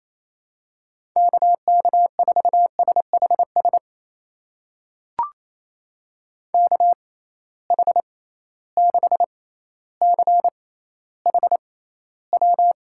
Morse code, heard as a keyed tone, about 28 wpm, spaced as for 14 wpm, 700 Hz. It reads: KK4SHS KH6CHW
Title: USA Call Signs - 6 Characters - 6x Character Spacing - 28wpm